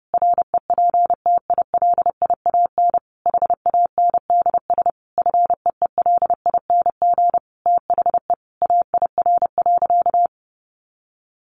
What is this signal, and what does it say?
Morse code, heard as a keyed tone, about 30 wpm, 700 Hz. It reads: REPTILIAN HANDS FEELING THE AIR.
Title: A Sound Of Thunder 30wpm